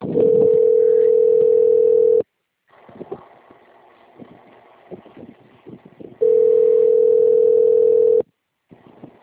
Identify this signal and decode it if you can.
Ringback tone (North America), 440 + 480 Hz.